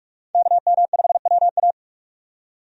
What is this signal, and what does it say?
Morse code, heard as a keyed tone, about 41 wpm, 700 Hz. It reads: KM5WA
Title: USA Call Signs - 5 Characters - 45wpm